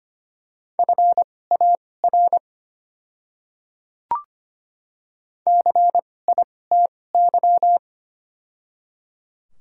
Morse code, heard as a keyed tone, about 25 wpm, spaced as for 18 wpm, 690 Hz. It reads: FAR CITY
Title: 2x Character Spacing - Top 300 Words - 25wpm